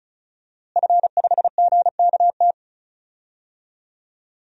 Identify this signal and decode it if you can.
Morse code, heard as a keyed tone, about 36 wpm, 700 Hz. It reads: F5GKT
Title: DXCC-100 Call Signs - 5 Characters - Repeated 3x - 35wpm